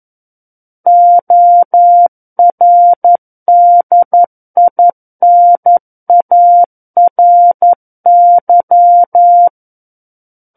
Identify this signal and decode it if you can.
Morse code, heard as a keyed tone, about 11 wpm, 700 Hz. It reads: ORDINARY
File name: 11 WPM